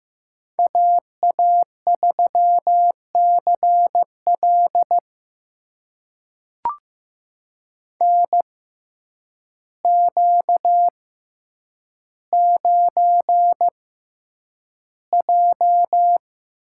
Morse code, heard as a keyed tone, about 15 wpm, spaced as for 8 wpm, 700 Hz. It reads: AA3CL NQ9J
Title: USA Call Signs - 6x Character Spacing - 15wpm